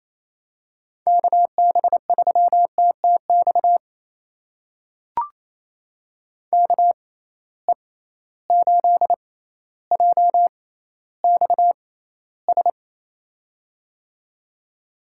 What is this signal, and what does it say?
Morse code, heard as a keyed tone, about 28 wpm, spaced as for 14 wpm, 700 Hz. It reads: KB3TTX KE8JXS
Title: USA Call Signs - 6 Characters - 6x Character Spacing - 28wpm